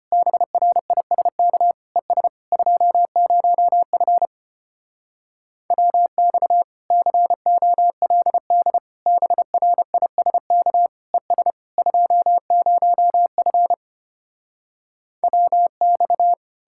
Morse code, heard as a keyed tone, about 33 wpm, 700 Hz. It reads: BRISK ES 20F WX COLD BRISK ES 20F WX